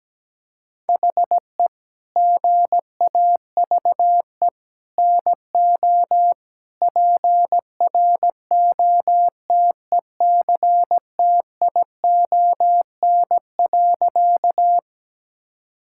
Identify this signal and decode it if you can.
Morse code, heard as a keyed tone, about 17 wpm, 700 Hz. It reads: HE GAVE NO PROTECTION.
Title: Sentences from Top 1600 Words 17wpm